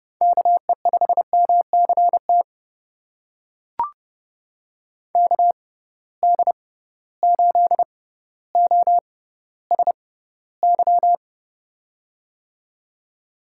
Morse code, heard as a keyed tone, about 30 wpm, spaced as for 15 wpm, 700 Hz. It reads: KE5MCT KD8OSY
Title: USA Call Signs - 6x Character Spacing - 30wpm